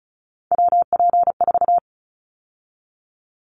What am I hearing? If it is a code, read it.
Morse code, heard as a keyed tone, about 34 wpm, 700 Hz. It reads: WP4
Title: Callsign Prefixes for Top 25 DXCC - 35wpm